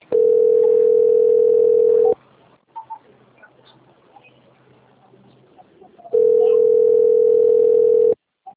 Ringback tone (North America), 440 + 480 Hz.
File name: rg-1600-unknown- - - .27547